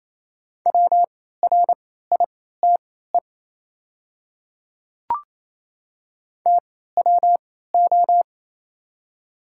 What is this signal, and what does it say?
Morse code, heard as a keyed tone, about 28 wpm, spaced as for 16 wpm, 700 Hz. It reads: WRITE TWO
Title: 3x Character Spacing - Top 100 Words - 28wpm